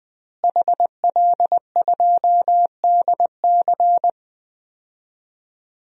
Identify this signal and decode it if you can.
Morse code, heard as a keyed tone, about 20 wpm, 700 Hz. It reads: HL2DC
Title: DXCC-100 Call Signs - 5 Characters - Repeated 2x - 20wpm